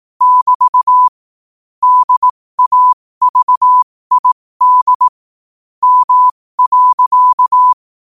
Morse code, heard as a keyed tone, about 18 wpm, spaced as for 15 wpm, 1000 Hz. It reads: <BT> DAVID M.
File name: qotdfiles / 15wpm